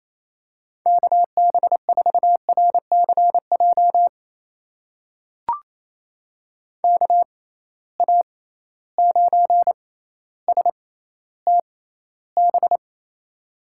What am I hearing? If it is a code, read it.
Morse code, heard as a keyed tone, about 28 wpm, spaced as for 14 wpm, 700 Hz. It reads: KB4RCJ KA9STB
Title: USA Call Signs - 6 Characters - 6x Character Spacing - 28wpm